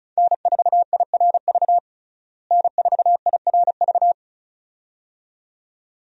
Morse code, heard as a keyed tone, about 36 wpm, 700 Hz. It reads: N4IRV N4IRV
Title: USA Call Signs - 5 Characters - Repeated 2x - 35wpm